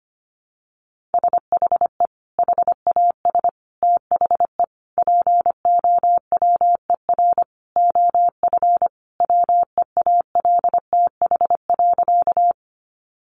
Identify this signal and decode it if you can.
Morse code, heard as a keyed tone, about 25 wpm, 700 Hz. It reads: SHE HAS THE POWER OF WEALTH.
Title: Sentences from Top 1200 Words 25wpm